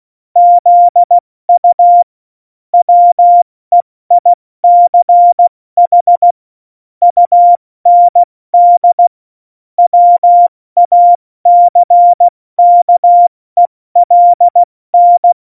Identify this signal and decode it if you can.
Morse code, heard as a keyed tone, about 16 wpm, spaced as for 14 wpm, 700 Hz. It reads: ZU WEICH UND WACKELN
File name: Morsekurs / Witze / Witz 26. Januar 2021, 70 F80